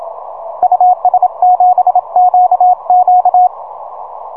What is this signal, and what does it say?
Morse code, heard as a keyed tone, about 27 wpm, 730 Hz, with radio noise. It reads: US7QQ
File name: simple-file-list / UN3GX 6M